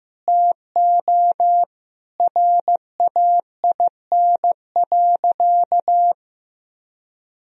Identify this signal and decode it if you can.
Morse code, heard as a keyed tone, about 15 wpm, 700 Hz. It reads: TO RAIN.
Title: Sentences from Top 1000 Words - Review Part 1 - 15wpm